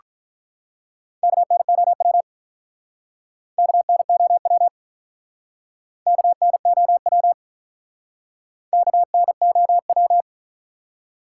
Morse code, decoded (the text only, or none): KNOW KNOW KNOW KNOW